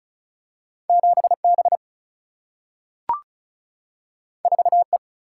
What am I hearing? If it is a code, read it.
Morse code, heard as a keyed tone, about 36 wpm, 700 Hz. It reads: 7 B 4 E
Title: Single-Letter Single-Number 35wpm